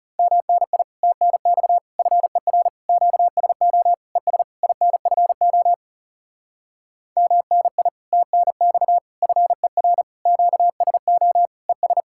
Morse code, heard as a keyed tone, about 37 wpm, 700 Hz. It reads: MNI TNX FER QSO ES INFO MNI TNX FER QSO ES
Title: Speed-Racing Common QSO Phrases Encore - 30wpm